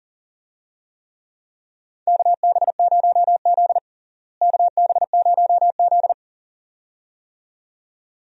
Morse code, heard as a keyed tone, about 40 wpm, 700 Hz. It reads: KB0Z KB0Z
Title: USA Call Signs - 4 Characters - Repeated 2x - 40wpm